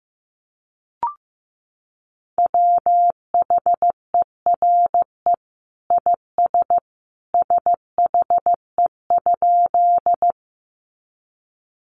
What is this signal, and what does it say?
Morse code, heard as a keyed tone, about 15 wpm, 700 Hz. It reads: WHERE IS SHE?
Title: Sentences from Top 200 Words 15wpm